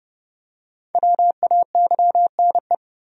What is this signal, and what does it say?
Morse code, heard as a keyed tone, about 29 wpm, 700 Hz. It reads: WAYNE
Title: USA Common Names - Max 5 Letters - 30wpm